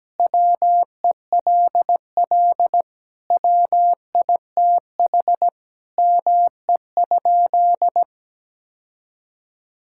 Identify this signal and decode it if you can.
Morse code, heard as a keyed tone, about 17 wpm, 700 Hz. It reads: WELL WITH ME?
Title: Sentences from Top 1600 Words 17wpm